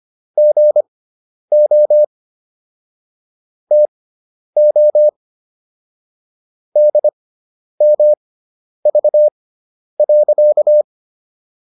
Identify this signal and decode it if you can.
Morse code, heard as a keyed tone, about 25 wpm, spaced as for 10 wpm, 600 Hz. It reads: GO TO DMV.